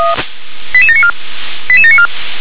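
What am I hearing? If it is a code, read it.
Five-tone selective call (ZVEI), heard as quick runs of five tones, tones of about 70 ms. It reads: 88794 88794
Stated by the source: ZVEI five-tone selective call received on SDR.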